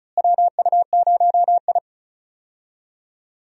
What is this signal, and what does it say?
Morse code, heard as a keyed tone, about 36 wpm, 700 Hz. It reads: WU0I
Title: USA Call Signs - 4 Characters - Repeated 3x - 35wpm